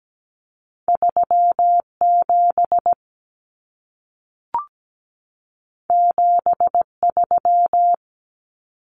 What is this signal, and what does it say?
Morse code, heard as a keyed tone, about 17 wpm, 700 Hz. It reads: 37 73
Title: Common Cut Numbers - 2-Digit 17wpm